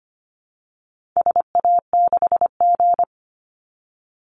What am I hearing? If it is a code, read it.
Morse code, heard as a keyed tone, about 25 wpm, 700 Hz. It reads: SA6G
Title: Call Signs Repeated 3x - 25wpm